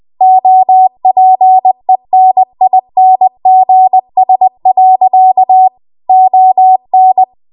Morse code, heard as a keyed tone, about 20 wpm, 750 Hz. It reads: OPENINGS. ON
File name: Archive / 20 WPM / 20WPM